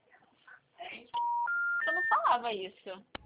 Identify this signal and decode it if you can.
Special information tone, with speech.